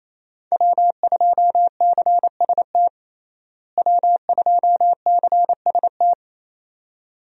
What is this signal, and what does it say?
Morse code, heard as a keyed tone, about 27 wpm, 700 Hz. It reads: W2CST W2CST
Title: USA Call Signs - 5 Characters - Repeated 2x - 28wpm